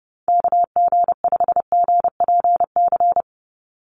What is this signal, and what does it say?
Morse code, heard as a keyed tone, about 30 wpm, 700 Hz. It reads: KG5GPC
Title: USA Call Signs 30wpm